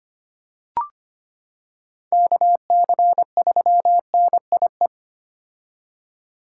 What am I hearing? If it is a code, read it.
Morse code, heard as a keyed tone, about 25 wpm, 700 Hz. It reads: KC3NIE
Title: K1USN SST QSO Practice - 25wpm